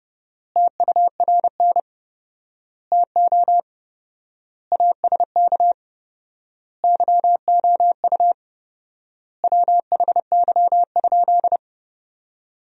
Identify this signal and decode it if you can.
Morse code, heard as a keyed tone, about 30 wpm, 700 Hz. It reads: TURN TO ASK YOU WHY?
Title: 4x Word-Spacing - Sentences from Top 200 Words - 30wpm